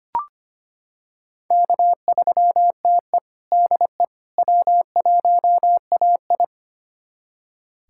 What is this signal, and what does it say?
Morse code, heard as a keyed tone, about 25 wpm, 700 Hz. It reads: K3TE DE W1AI
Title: USA Callsign DE Callsign - 4 Characters - 25wpm